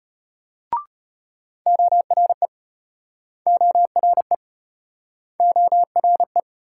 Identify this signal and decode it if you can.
Morse code, heard as a keyed tone, about 35 wpm, 700 Hz. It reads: ORE ORE ORE